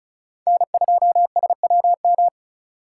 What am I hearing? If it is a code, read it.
Morse code, heard as a keyed tone, about 36 wpm, 700 Hz. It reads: N2SWM
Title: USA Call Signs - 5 Characters - 4x Character Spacing - 35wpm